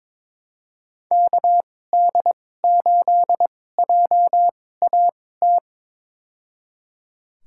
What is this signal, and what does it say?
Morse code, heard as a keyed tone, about 22 wpm, spaced as for 16 wpm, 700 Hz. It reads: KD8JAT
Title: USA Call Signs - 2x Character Spacing - 22wpm